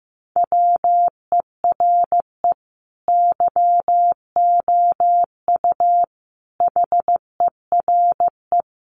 Morse code, heard as a keyed tone, about 15 wpm, 700 Hz. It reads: WERE YOU HERE
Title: Sentences from Top 200 Words 15wpm